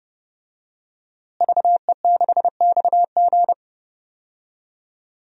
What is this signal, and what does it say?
Morse code, heard as a keyed tone, about 32 wpm, 700 Hz. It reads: VE6XG
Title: DXCC-100 Call Signs - 5 Characters - Repeated 2x - 30wpm